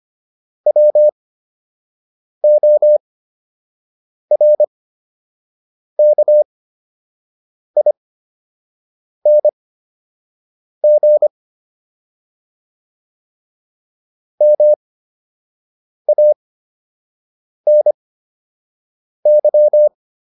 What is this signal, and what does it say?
Morse code, heard as a keyed tone, about 25 wpm, spaced as for 6 wpm, 600 Hz. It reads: WORKING MANY